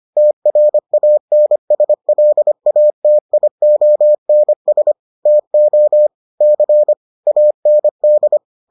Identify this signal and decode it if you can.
Morse code, heard as a keyed tone, about 25 wpm, 600 Hz. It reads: TRANSLATIONS TO C AND